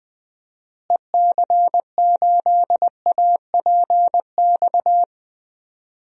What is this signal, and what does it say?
Morse code, heard as a keyed tone, about 20 wpm, 700 Hz. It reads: EC8APX
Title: DXCC-100 Call Signs - 6 Characters - Repeated 3x - 20wpm